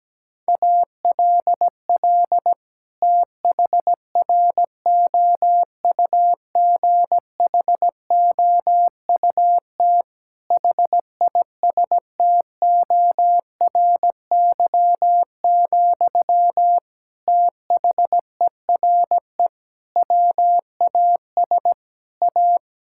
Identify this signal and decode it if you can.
Morse code, heard as a keyed tone, about 17 wpm, 700 Hz. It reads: ALL THROUGHOUT HISTORY, THERE WAS A
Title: Sentences from Top 1000 Words - Review Part 5 - 17wpm